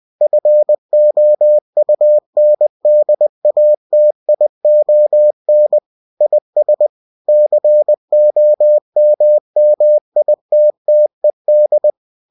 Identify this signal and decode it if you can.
Morse code, heard as a keyed tone, about 20 wpm, 600 Hz. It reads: FOUNDATION IS COMMITTED
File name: cw-books / winnie-the-pooh / winnie-20wpm-page-0121